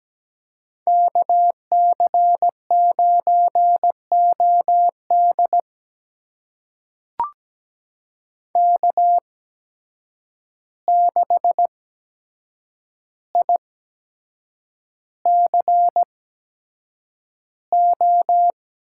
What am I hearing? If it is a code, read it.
Morse code, heard as a keyed tone, about 17 wpm, spaced as for 7 wpm, 700 Hz. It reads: KC9OD K6ICO
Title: USA Call Signs - 5 Characters - 8x Character Spacing - 17wpm